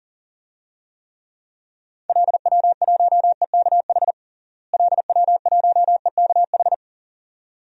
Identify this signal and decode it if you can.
Morse code, heard as a keyed tone, about 40 wpm, 700 Hz. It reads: LW1EKH LW1EKH
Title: DXCC-100 Call Signs - 6 Characters - Repeated 2x - 40wpm